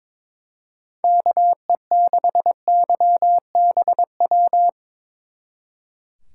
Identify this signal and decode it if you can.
Morse code, heard as a keyed tone, about 25 wpm, 700 Hz. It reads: KE6YBW
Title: Speed Racing - USA 2x3 Call Signs 22wpm